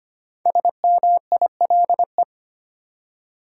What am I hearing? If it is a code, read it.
Morse code, heard as a keyed tone, about 25 wpm, 700 Hz. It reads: SMILE
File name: 5 Letter Words - First 3 Letters Given - 25wpm